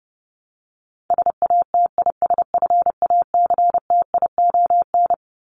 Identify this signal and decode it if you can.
Morse code, heard as a keyed tone, about 30 wpm, 700 Hz. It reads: SATISFACTION
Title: Top 1751-2000 Words 30wpm